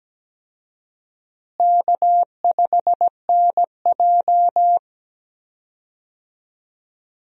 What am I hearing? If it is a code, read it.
Morse code, heard as a keyed tone, about 17 wpm, 700 Hz. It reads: K5NJ
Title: USA Call Signs - 4 Characters - Repeated 3x - 17wpm